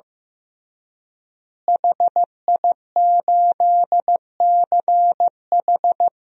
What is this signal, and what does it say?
Morse code, heard as a keyed tone, about 15 wpm, 700 Hz. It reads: HI8CH
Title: DXCC-100 Call Signs - 5 Characters - Repeated 3x - 15wpm